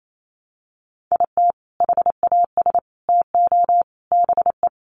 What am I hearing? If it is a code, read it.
Morse code, heard as a keyed tone, about 28 wpm, 700 Hz. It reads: IT HAS TO BE
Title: Sentences from Top 500 Words 28wpm